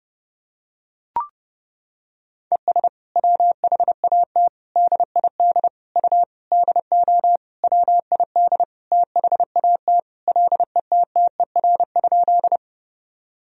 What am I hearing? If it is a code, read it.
Morse code, heard as a keyed tone, about 30 wpm, 700 Hz. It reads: ES WHAT DID U DO WID THAT LETTER?
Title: Sentences from Top 300 Words and Abbreviations 30wpm